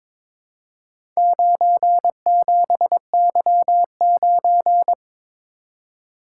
Morse code, heard as a keyed tone, about 22 wpm, 700 Hz. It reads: 97Y9
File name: practice / 22wpm / Four Letter-Number Combinations - 22wpm